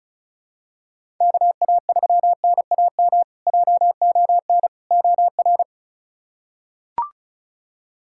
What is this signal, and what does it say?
Morse code, heard as a keyed tone, about 34 wpm, 700 Hz. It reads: KA3NAM JON OR
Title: K1USN SST QSO Practice - 35wpm